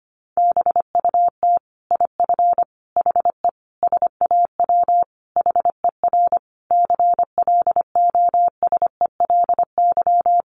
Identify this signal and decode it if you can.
Morse code, heard as a keyed tone, about 25 wpm, 700 Hz. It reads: BUT IF HE SAW HER CLOSELY
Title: Sentences from Top 1400 Words 25wpm